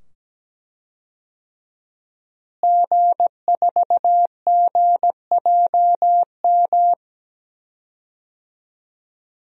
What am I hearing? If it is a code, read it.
Morse code, heard as a keyed tone, about 17 wpm, 700 Hz. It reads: G4GJM
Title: DXCC-100 Call Signs - 5 Characters - Repeated 2x - 17wpm